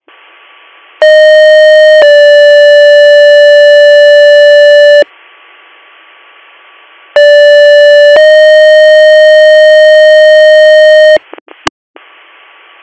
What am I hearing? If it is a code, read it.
Two-tone page, heard as a one-second tone, then a three-second tone, about 620 then 580 Hz, about 580 then 620 Hz, with noise. It is heard from 1.0 to 11.2 s.